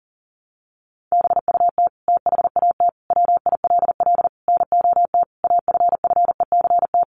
Morse code, heard as a keyed tone, about 40 wpm, 700 Hz. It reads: BUT THAT WILL NOT AFFECT
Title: Sentences from Top 900 Words 40wpm